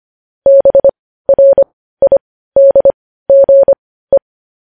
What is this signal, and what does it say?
Morse code, heard as a keyed tone, about 25 wpm, spaced as for 15 wpm, 550 Hz. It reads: BRIDGE